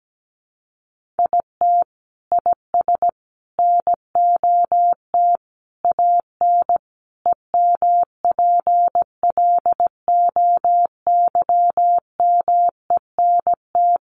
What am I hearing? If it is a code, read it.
Morse code, heard as a keyed tone, about 17 wpm, 700 Hz. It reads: IT IS NOT AN EMPLOYMENT
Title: Sentences from Top 1500 Words 17wpm